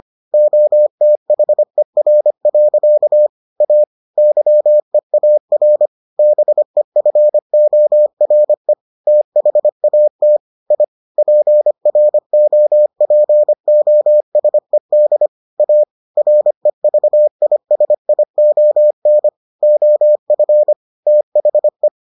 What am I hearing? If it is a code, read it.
Morse code, heard as a keyed tone, about 25 wpm, 600 Hz. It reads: OTHER. A YEAR BEFORE THAT I PROPOSED A REVISION OF THE